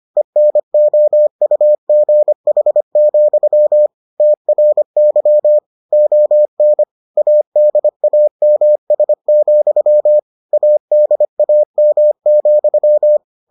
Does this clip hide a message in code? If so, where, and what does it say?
Yes: Morse code from 0.2 to 13.2 s, heard as a keyed tone, about 25 wpm, 600 Hz. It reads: ENOUGH, TRY ON ADAMS, ADAM,